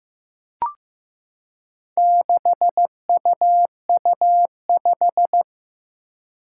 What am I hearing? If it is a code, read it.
Morse code, heard as a keyed tone, about 15 wpm, 700 Hz. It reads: 6UU5
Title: Four Challenging Letter-Number Combinations 15wpm